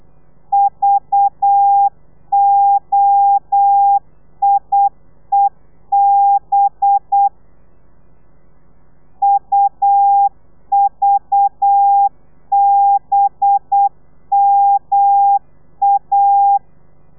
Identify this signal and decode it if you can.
Morse code, heard as a keyed tone, about 8 wpm, 800 Hz. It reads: VOIEB UVBMA
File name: telegrafia / i2tfj / Lez4 35